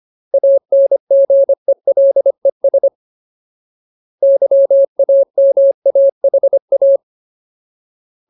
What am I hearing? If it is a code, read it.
Morse code, heard as a keyed tone, about 25 wpm, 550 Hz. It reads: ANGELES YAMAHA
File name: Ultimate Word List / 25WPM Ultimate